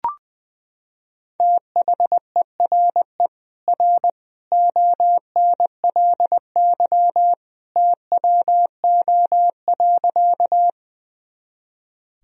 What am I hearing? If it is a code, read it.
Morse code, heard as a keyed tone, about 21 wpm, 700 Hz. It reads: THERE R ONLY TWO.
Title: Sentences from Top 200 Words and Abbreviations 20wpm